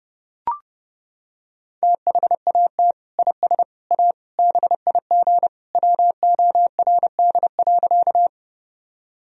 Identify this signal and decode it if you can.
Morse code, heard as a keyed tone, about 32 wpm, 700 Hz. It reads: THAT IS A BIG WORD.
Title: Sentences from Top 1000 Words - Review Part 1 - 30wpm